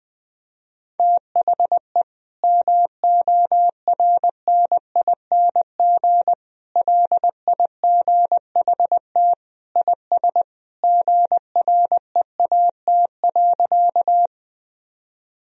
Morse code, heard as a keyed tone, about 20 wpm, 700 Hz. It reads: THE MORNING LIGHT IS GREAT.
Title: Sentences from Top 1000 Words - Review Part 1 - 20wpm